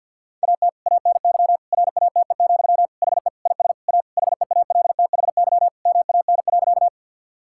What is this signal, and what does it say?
Morse code, heard as a keyed tone, about 51 wpm, 700 Hz. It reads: AT ANY RATE, HE IS A HEALTHY MAN.
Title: Sentences from Top 1000 Words - Review Part 5 - 50wpm